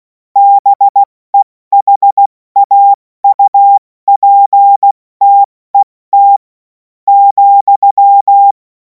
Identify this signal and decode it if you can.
Morse code, heard as a keyed tone, about 16 wpm, spaced as for 13 wpm, 800 Hz. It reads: BEHAUPTET,
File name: Morsekurs / Witze / Witz 10. Februar 2026 70 F80